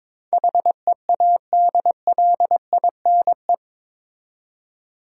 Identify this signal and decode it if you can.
Morse code, heard as a keyed tone, about 22 wpm, 700 Hz. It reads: HEADLINE
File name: practice / 22wpm / Eight Letter Words - 22wpm